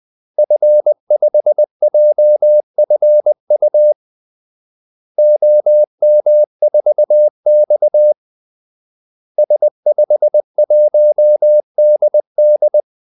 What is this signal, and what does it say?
Morse code, heard as a keyed tone, about 20 wpm, 600 Hz. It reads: F5JFU OM4X S51DD
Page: Kaldesignaler 20 wpm